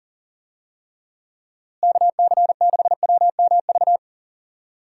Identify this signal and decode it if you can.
Morse code, heard as a keyed tone, about 38 wpm, 700 Hz. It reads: KC6WMV
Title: USA Call Signs - 6 Characters - Repeated 3x - 40wpm